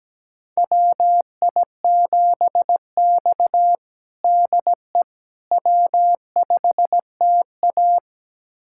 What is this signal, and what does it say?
Morse code, heard as a keyed tone, about 17 wpm, 700 Hz. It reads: WI7X DE W5TA
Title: USA Callsign DE Callsign - 4 Characters - 17wpm